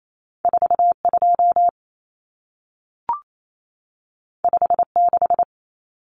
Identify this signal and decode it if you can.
Morse code, heard as a keyed tone, about 28 wpm, 700 Hz. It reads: 42 56
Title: Numbers - 2-Digit 28wpm